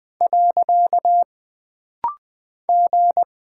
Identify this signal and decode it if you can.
Morse code, heard as a keyed tone, about 20 wpm, 700 Hz. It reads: . G
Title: Single Character (Letter:Number:Punctuation:Prosign) - Rapid-Fire - 20wpm